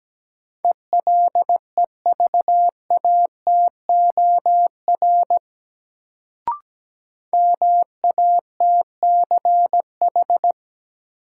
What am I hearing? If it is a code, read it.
Morse code, heard as a keyed tone, about 17 wpm, 700 Hz. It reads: ELEVATOR MATCH
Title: Sentences from Top 1600 Words - New Word Drill - 17wpm